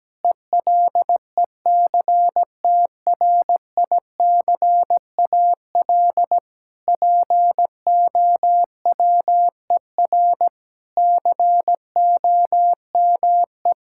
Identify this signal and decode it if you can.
Morse code, heard as a keyed tone, about 17 wpm, 700 Hz. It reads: ELECTRICAL POWER COME
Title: Sentences from Top 1000 Words - Review Part 6 - 17wpm